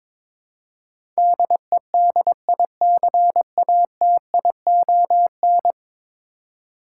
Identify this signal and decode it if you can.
Morse code, heard as a keyed tone, about 22 wpm, 700 Hz. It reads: DEDICATION
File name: practice / 22wpm / 10 Letter Words - First 3 Letters Given - 22wpm